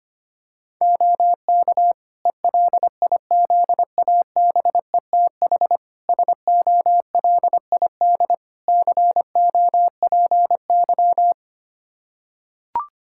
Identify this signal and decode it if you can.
Morse code, heard as a keyed tone, about 25 wpm, 700 Hz. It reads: OK ELIZABETH SOLID COPY
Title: Practice QSO - Exchange 3 - More Info - 25wpm